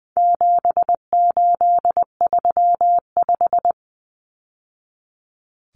Morse code, heard as a keyed tone, about 20 wpm, 700 Hz. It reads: 7835
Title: Numbers - 4-Digit 20wpm